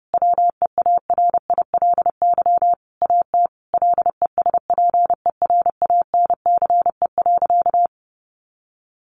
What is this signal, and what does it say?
Morse code, heard as a keyed tone, about 30 wpm, 700 Hz. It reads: WEARILY AT LESPERANCE.
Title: A Sound Of Thunder 30wpm